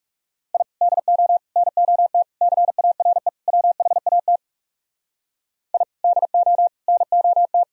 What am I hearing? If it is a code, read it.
Morse code, heard as a keyed tone, about 37 wpm, 700 Hz. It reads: I DO NOT CARE WHAT I DO NOT
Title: Speed-Racing - Sets of 5 Words 30wpm